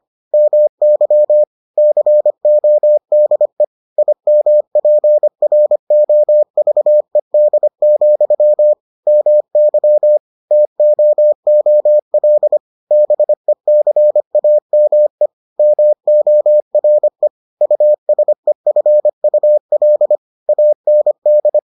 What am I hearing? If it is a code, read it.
Morse code, heard as a keyed tone, about 25 wpm, 600 Hz. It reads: MY CODE IMPROVED, MY TOOL BECAME MORE USEFUL AND